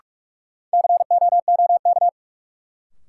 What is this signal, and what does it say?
Morse code, heard as a keyed tone, about 44 wpm, 700 Hz. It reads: COOK
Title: Sentences from Top 700 Words - New Word Drill - 45wpm